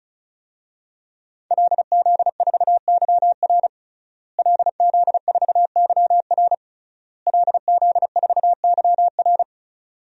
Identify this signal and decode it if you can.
Morse code, heard as a keyed tone, about 34 wpm, 700 Hz. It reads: LZ4YR LZ4YR LZ4YR
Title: DXCC-100 Call Signs - 5 Characters - Repeated 3x - 35wpm